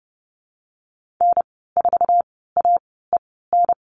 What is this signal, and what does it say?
Morse code, heard as a keyed tone, about 30 wpm, spaced as for 17 wpm, 700 Hz. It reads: N4AEN
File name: Lesson 14g - Number 2 - Call Signs 30wpm